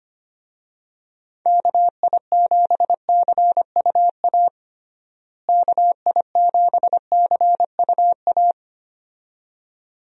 Morse code, heard as a keyed tone, about 25 wpm, 700 Hz. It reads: KI7CUA KI7CUA
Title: USA Call Signs - 6 Characters - Repeated 2x - 25wpm